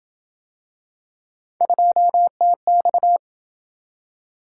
Morse code, heard as a keyed tone, about 27 wpm, 700 Hz. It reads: 2TX